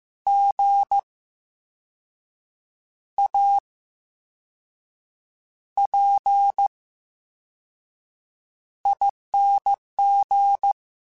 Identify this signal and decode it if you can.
Morse code, heard as a keyed tone, about 15 wpm, spaced as for 5 wpm, 800 Hz. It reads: GAPING
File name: Special INGs / Special INGs-mp3s / Slow ing @13